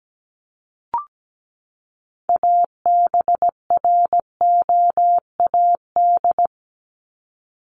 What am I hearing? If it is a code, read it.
Morse code, heard as a keyed tone, about 17 wpm, 700 Hz. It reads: ABROAD
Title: Top 1751-2000 Words 17wpm